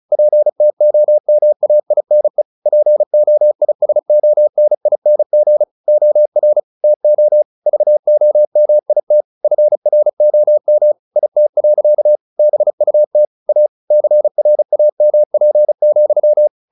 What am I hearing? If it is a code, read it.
Morse code, heard as a keyed tone, about 34 wpm, 600 Hz. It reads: PTOMAINE POISONING OR TO VOMIT FROM IT. BUT A CRAMP,
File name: the old man and the sea / the old man and the sea 1 35wpm